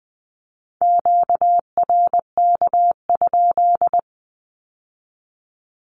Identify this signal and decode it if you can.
Morse code, heard as a keyed tone, about 20 wpm, 700 Hz. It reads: QRK?
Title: Q-Codes with Spoken Meaning - 20wpm